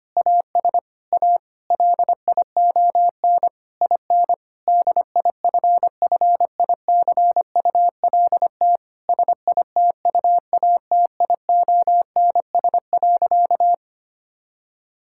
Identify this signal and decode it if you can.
Morse code, heard as a keyed tone, about 25 wpm, 700 Hz. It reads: AS A LION IN DIFFICULT SITUATIONS.
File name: practice / 25wpm / Difficult Sentences with Hints - 25wpm